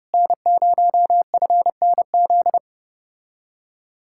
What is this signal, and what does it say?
Morse code, heard as a keyed tone, about 33 wpm, 700 Hz. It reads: N0FNZ